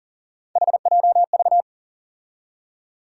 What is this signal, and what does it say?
Morse code, heard as a keyed tone, about 40 wpm, 700 Hz. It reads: HJV
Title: USA Call Sign Postfixes - 3 Letters 40wpm